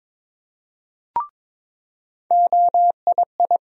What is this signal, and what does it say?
Morse code, heard as a keyed tone, about 22 wpm, 700 Hz. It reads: OII
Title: USA Call Sign Postfixes - 3 Letters 22wpm